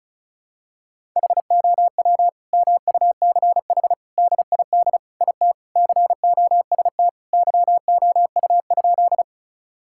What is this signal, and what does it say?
Morse code, heard as a keyed tone, about 36 wpm, 700 Hz. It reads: HOW MUCH DID IT COST YOU?
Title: Sentences from Top 1000 Words - Review Part 5 - 35wpm